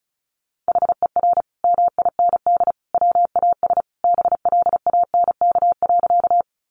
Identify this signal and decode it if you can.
Morse code, heard as a keyed tone, about 34 wpm, 700 Hz. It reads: HER MIND WAS BLANK.
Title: Sentences from Top 1500 Words 35wpm